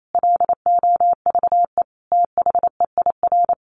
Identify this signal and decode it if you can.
Morse code, heard as a keyed tone, about 28 wpm, 700 Hz. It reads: LOVE THEIR
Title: Sentences from Top 1500 Words 28wpm